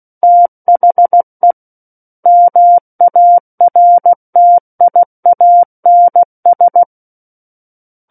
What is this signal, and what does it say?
Morse code, heard as a keyed tone, about 16 wpm, 700 Hz. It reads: THE MARTIANS
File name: Book One / 07-Chapter 7 - 16 WPM